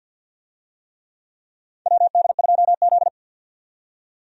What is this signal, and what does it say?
Morse code, heard as a keyed tone, about 47 wpm, 700 Hz. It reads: WD2Z
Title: USA Call Signs - 4 Characters - 50wpm